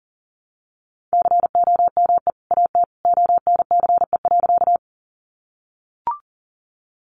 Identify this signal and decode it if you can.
Morse code, heard as a keyed tone, about 40 wpm, 700 Hz. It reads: COME AT ONCE.
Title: Sentences from Top 300 Words 40wpm